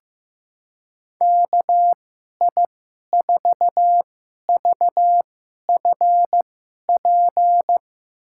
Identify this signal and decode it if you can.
Morse code, heard as a keyed tone, about 15 wpm, spaced as for 11 wpm, 700 Hz. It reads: KI4VFP
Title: USA Call Signs - 2x Character Spacing - 15wpm